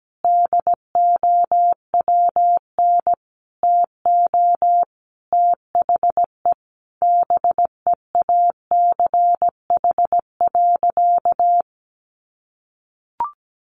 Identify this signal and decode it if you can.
Morse code, heard as a keyed tone, about 17 wpm, 700 Hz. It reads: DOWN TO THE BEACH.
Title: Sentences from Top 1500 Words 17wpm